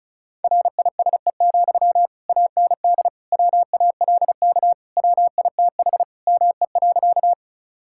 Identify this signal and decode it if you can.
Morse code, heard as a keyed tone, about 36 wpm, 700 Hz. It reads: RISE, AND WALK WITH ME.
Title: Sentences from Top 1000 Words - Review Part 6 - 35wpm